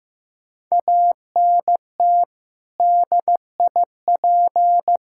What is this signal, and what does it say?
Morse code, heard as a keyed tone, about 15 wpm, 700 Hz. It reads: ANT DIP
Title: K1USN SST QSO Practice 15wpm